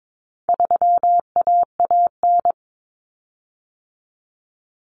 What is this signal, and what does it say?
Morse code, heard as a keyed tone, about 22 wpm, 700 Hz. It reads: 3AAN
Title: Common Cut Numbers - 4-Digit 22wpm